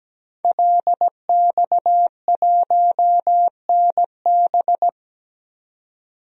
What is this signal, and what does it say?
Morse code, heard as a keyed tone, about 17 wpm, 700 Hz. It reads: LX1NB
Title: DXCC-100 Call Signs - 5 Characters - 17wpm